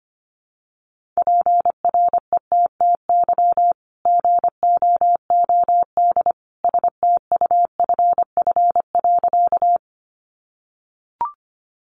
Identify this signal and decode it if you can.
Morse code, heard as a keyed tone, about 25 wpm, 700 Hz. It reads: PRETTY GOOD STUFF.
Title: Sentences from Top 1400 Words 25wpm